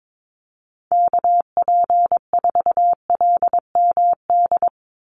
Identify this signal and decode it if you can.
Morse code, heard as a keyed tone, about 22 wpm, 700 Hz. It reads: KP4LMD
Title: USA Call Signs 22wpm